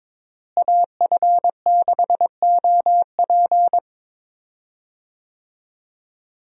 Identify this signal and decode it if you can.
Morse code, heard as a keyed tone, about 21 wpm, 700 Hz. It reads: AF6OP
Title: USA Call Signs - 5 Characters - Repeated 2x - 22wpm